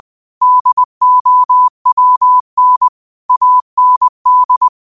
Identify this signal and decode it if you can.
Morse code, heard as a keyed tone, about 20 wpm, 1000 Hz. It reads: DOWN AND
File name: qotdfiles / 20wpm